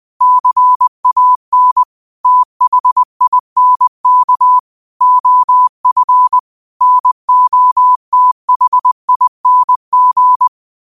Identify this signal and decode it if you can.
Morse code, heard as a keyed tone, about 20 wpm, 1000 Hz. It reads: CAN THINK OF NOTHING